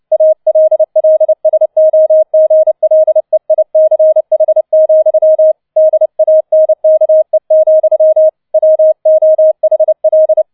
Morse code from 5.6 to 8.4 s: DANKE,